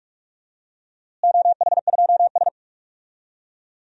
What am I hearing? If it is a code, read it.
Morse code, heard as a keyed tone, about 46 wpm, 700 Hz. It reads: OH2S
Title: DXCC-100 Call Signs - 4 Characters - Repeated 2x - 45wpm